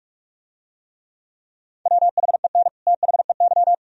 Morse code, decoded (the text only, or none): WHEN THEY